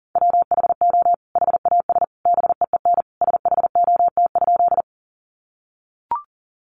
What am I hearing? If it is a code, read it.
Morse code, heard as a keyed tone, about 40 wpm, 700 Hz. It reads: WHO HAS BEEN SHOT?
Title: Sentences from Top 1100 Words 40wpm